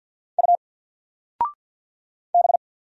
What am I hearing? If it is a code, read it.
Morse code, heard as a keyed tone, about 48 wpm, 700 Hz. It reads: U B
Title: Single Letter-Number Alt 50wpm